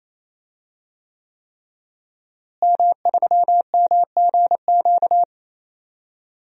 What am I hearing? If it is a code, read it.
Morse code, heard as a keyed tone, about 28 wpm, 700 Hz. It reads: M3MGQ
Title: DXCC-100 Call Signs - 5 Characters - Repeated 3x - 28wpm